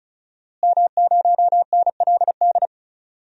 Morse code, heard as a keyed tone, about 36 wpm, 700 Hz. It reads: M0NLD
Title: DXCC-100 Call Signs - 5 Characters - Repeated 2x - 35wpm